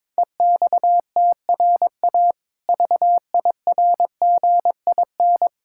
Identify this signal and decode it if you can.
Morse code, heard as a keyed tone, about 22 wpm, 700 Hz. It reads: EXTRA VIRGIN
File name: Difficult Words with Hints - 22wpm